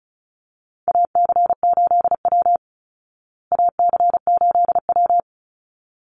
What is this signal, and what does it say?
Morse code, heard as a keyed tone, about 34 wpm, 700 Hz. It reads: AC8W AC8W
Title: Call Signs Repeated 3x - 35wpm